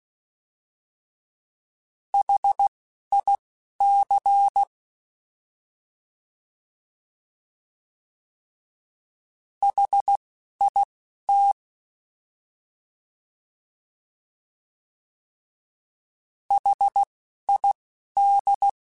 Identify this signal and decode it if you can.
Morse code, heard as a keyed tone, about 16 wpm, spaced as for 11 wpm, 800 Hz. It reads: HIC HIT HID
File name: Alphabetical / Alphabetical mp3s / Hs @10